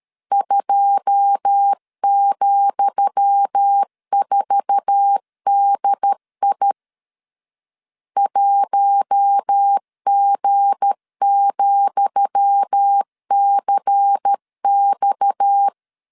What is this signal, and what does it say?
Morse code, heard as a keyed tone, about 13 wpm, 800 Hz. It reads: 2,4DI 1G,CX
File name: LEZIONI / 10 SIGNS / 10D COMA 12wpm 2544